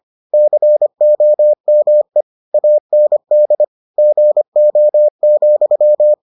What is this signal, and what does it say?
Morse code, heard as a keyed tone, about 25 wpm, 600 Hz. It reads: COME AND GO,